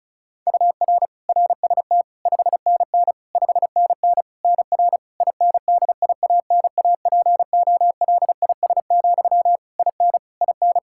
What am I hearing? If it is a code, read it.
Morse code, heard as a keyed tone, about 35 wpm, 700 Hz. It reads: UR RST 5NN 5NN NR INDIANAPOLIS, IN IN
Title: K1USN SST QSO Practice 35wpm